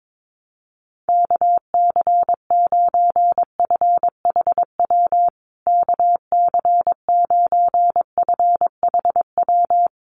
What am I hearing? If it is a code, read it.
Morse code, heard as a keyed tone, about 22 wpm, 700 Hz. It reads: KC9FHW KC9FHW
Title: USA Call Signs Repeated 3x 22wpm